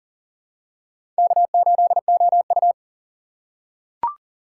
Morse code, heard as a keyed tone, about 40 wpm, 700 Hz. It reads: K8OU
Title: USA Call Signs - 4 Characters - 6x Character Spacing - 40wpm